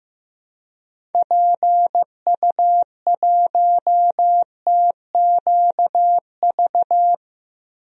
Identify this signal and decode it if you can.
Morse code, heard as a keyed tone, about 15 wpm, 700 Hz. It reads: PU1TQV
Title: DXCC-100 Call Signs - 6 Characters - 15wpm